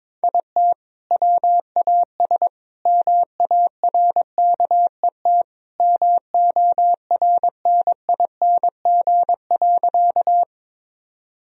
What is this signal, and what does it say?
Morse code, heard as a keyed tone, about 22 wpm, 700 Hz. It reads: IT WAS MARKET MORNING.
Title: Sentences from Top 1000 Words - Review Part 3 - 22wpm